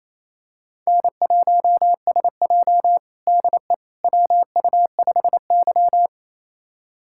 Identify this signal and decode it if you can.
Morse code, heard as a keyed tone, about 28 wpm, 700 Hz. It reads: N1SJ DE WU5Y
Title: USA Callsign DE Callsign - 4 Characters - 28wpm